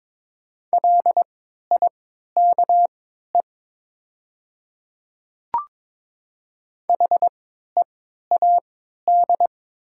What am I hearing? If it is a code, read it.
Morse code, heard as a keyed tone, about 22 wpm, spaced as for 12 wpm, 700 Hz. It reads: LIKE HEAD
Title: 3x Character Spacing - Top 300 Words - 22wpm